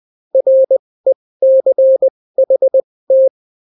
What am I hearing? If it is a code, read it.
Morse code, heard as a keyed tone, about 20 wpm, spaced as for 16 wpm, 530 Hz. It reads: RECHT